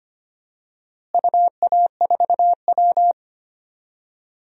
Morse code, heard as a keyed tone, about 25 wpm, 700 Hz. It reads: UA4W
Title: DXCC-100 Call Signs - 4 Characters - Repeated 2x - 25wpm